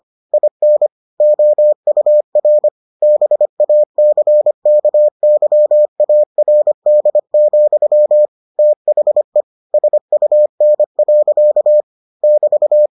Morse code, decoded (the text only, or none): IN OUR BACKYARD, THE SUN. <BT>